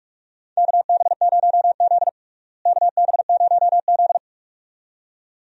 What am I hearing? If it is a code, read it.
Morse code, heard as a keyed tone, about 46 wpm, 700 Hz. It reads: KB0Z KB0Z
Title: USA Call Signs - 4 Characters - Repeated 2x - 45wpm